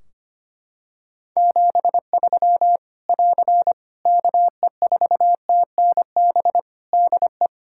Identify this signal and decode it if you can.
Morse code, heard as a keyed tone, about 25 wpm, 700 Hz. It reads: 73 <AR> KE4TNB DE
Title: K1USN SST QSO Practice 25wpm